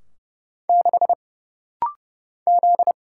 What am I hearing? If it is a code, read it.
Morse code, heard as a keyed tone, about 30 wpm, 700 Hz. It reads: 6 Z
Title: Single Letter-Number - Rapid-Fire 30wpm